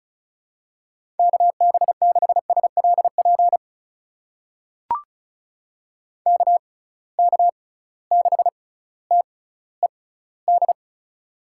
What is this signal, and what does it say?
Morse code, heard as a keyed tone, about 36 wpm, spaced as for 18 wpm, 700 Hz. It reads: KB6SLP KK6TED
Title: USA Call Signs - 6x Character Spacing - 35wpm